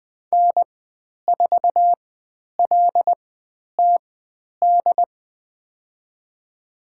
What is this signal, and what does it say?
Morse code, heard as a keyed tone, about 20 wpm, spaced as for 10 wpm, 700 Hz. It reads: N4LTD
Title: 102 Lesson 17g - Letter u - Call Signs 20wpm